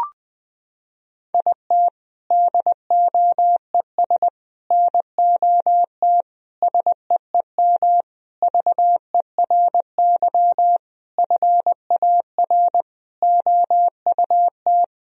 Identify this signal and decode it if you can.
Morse code, heard as a keyed tone, about 20 wpm, 700 Hz. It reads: IT DOES NOT SEEM VERY FAR OUT
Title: Sentences from Top 1000 Words - Review Part 1 - 20wpm